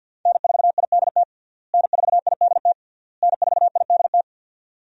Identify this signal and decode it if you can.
Morse code, heard as a keyed tone, about 46 wpm, 700 Hz. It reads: N4IDT N4IDT N4IDT
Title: USA Call Signs - 5 Characters - Repeated 3x - 50wpm